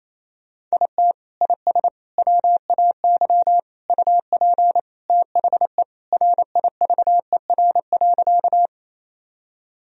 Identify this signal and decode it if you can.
Morse code, heard as a keyed tone, about 28 wpm, 700 Hz. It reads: IT IS WAY UP THE RIVER.
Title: Sentences from Top 1000 Words - Review Part 1 - 28wpm